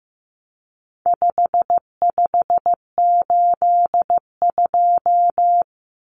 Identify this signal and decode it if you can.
Morse code, heard as a keyed tone, about 15 wpm, 700 Hz. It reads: 5582
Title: Numbers - 4-Digit 15wpm